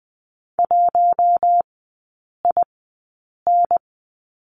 Morse code, heard as a keyed tone, about 20 wpm, spaced as for 8 wpm, 700 Hz. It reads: 1IN
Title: 041 Lesson 8e - Number 1 - Three Characters 20wpm